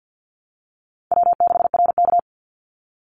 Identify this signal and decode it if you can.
Morse code, heard as a keyed tone, about 50 wpm, 700 Hz. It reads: W6RK
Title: Call Signs Repeated 3x - 50wpm